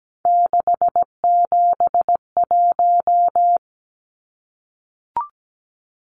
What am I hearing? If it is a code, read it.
Morse code, heard as a keyed tone, about 17 wpm, 700 Hz. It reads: 671
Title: Numbers - 3-Digit 17wpm